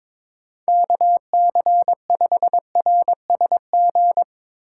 Morse code, heard as a keyed tone, about 23 wpm, 700 Hz. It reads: KC5RSG
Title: Speed Racing - USA 2x3 Call Signs 22wpm